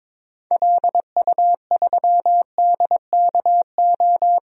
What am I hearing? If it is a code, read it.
Morse code, heard as a keyed tone, about 22 wpm, 700 Hz. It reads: LU3DKO
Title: DXCC-100 Call Signs - 6 Characters - 22wpm